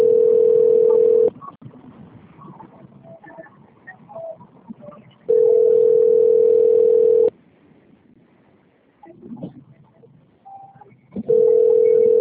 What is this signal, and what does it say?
Ringback tone (North America), 440 + 480 Hz.